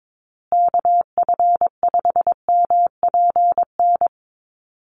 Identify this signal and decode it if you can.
Morse code, heard as a keyed tone, about 22 wpm, 700 Hz. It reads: KF5MPN
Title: USA Call Signs Repeated 2x 22wpm